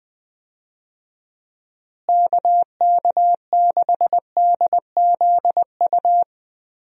Morse code, heard as a keyed tone, about 19 wpm, 700 Hz. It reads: KK6DZU
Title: USA Call Signs - 6 Characters - Repeated 3x - 20wpm